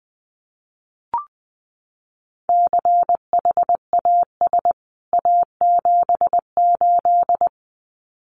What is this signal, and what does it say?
Morse code, heard as a keyed tone, about 19 wpm, 700 Hz. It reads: CHAS A78
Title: CWT Exchange 20wpm